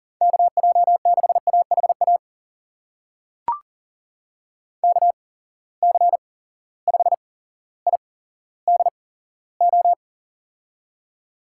Morse code, heard as a keyed tone, about 40 wpm, spaced as for 17 wpm, 700 Hz. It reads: KJ6AHA KC5IDO
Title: USA Call Signs - 6 Characters - 8x Character Spacing - 40wpm